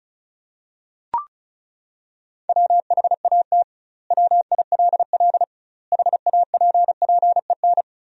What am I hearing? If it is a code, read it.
Morse code, heard as a keyed tone, about 35 wpm, 700 Hz. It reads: WHAT WILL HAPPEN
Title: 2x Word-Spacing - Sentences from Top 400 Words - 35wpm